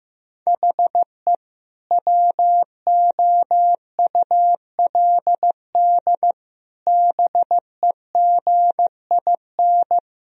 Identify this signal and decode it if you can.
Morse code, heard as a keyed tone, about 16 wpm, 700 Hz. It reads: HE WOULD BEGIN
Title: Sentences from Top 1000 Words - Review Part 1 - 15wpm